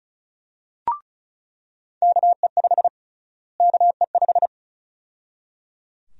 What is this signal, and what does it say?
Morse code, heard as a keyed tone, about 34 wpm, 700 Hz. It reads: KE5 KE5
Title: USA Call Sign Prefixes - Repeated 2x - 35wpm